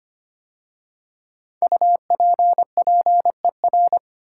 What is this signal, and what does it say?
Morse code, heard as a keyed tone, about 25 wpm, 700 Hz. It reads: UPPER
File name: 5 Letter Words - First 3 Letters Given - 25wpm